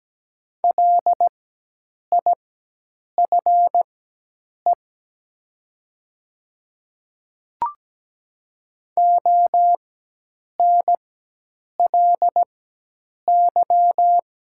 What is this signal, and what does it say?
Morse code, heard as a keyed tone, about 17 wpm, spaced as for 8 wpm, 700 Hz. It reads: LIFE ONLY
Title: 4x Character Spacing - Top 300 Words - 17wpm